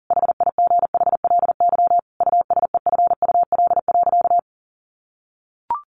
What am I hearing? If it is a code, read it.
Morse code, heard as a keyed tone, about 40 wpm, 700 Hz. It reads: HIGHLY USEFUL.
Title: Sentences from Top 1100 Words 40wpm